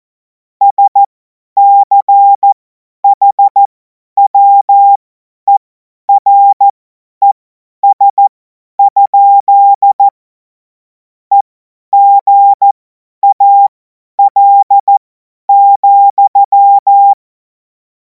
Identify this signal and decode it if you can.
Morse code, heard as a keyed tone, about 14 wpm, spaced as for 10 wpm, 800 Hz. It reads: SCHWERES? EGAL,
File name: Morsekurs / Witze / Witz 03. Februar 2026 50 F70